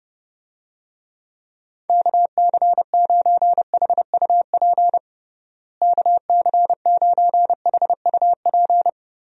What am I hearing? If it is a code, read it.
Morse code, heard as a keyed tone, about 30 wpm, 700 Hz. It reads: KC9HUP KC9HUP
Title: USA Call Signs - 6 Characters - Repeated 3x - 30wpm